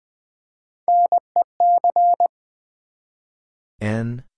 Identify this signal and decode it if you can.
Morse code, heard as a keyed tone, about 20 wpm, 700 Hz. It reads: NEC
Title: Build A Word - 1 Letter At A Time - 3 to 5-Letter Words - 20wpm